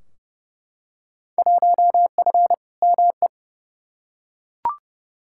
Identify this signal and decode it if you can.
Morse code, heard as a keyed tone, about 30 wpm, 700 Hz. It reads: 1F ME
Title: Field Day Exchanges Encore - 30wpm